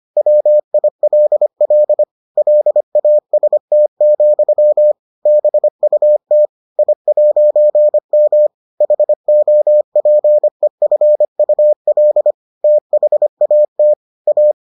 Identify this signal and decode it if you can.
Morse code, heard as a keyed tone, about 25 wpm, 600 Hz. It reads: WILL LAST, BUT I'M HOPEFUL THAT A